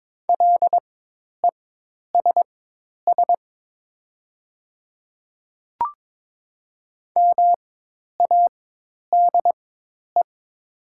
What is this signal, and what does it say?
Morse code, heard as a keyed tone, about 23 wpm, spaced as for 10 wpm, 700 Hz. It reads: LESS MADE
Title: 4x Character Spacing - Top 500 Words - 22wpm